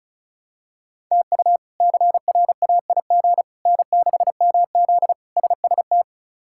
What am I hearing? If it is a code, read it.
Morse code, heard as a keyed tone, about 36 wpm, 700 Hz. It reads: TU CRAIG N6MZ SST
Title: K1USN SST QSO Practice - 35wpm